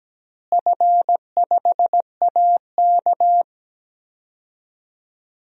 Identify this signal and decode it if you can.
Morse code, heard as a keyed tone, about 17 wpm, 700 Hz. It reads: F5AK
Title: DXCC-100 Call Signs - 4 Characters - Repeated 2x - 17wpm